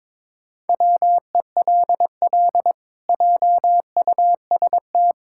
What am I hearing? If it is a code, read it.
Morse code, heard as a keyed tone, about 22 wpm, 700 Hz. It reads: WELL JUST
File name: practice / 22wpm / Sets of 7 Words - 22wpm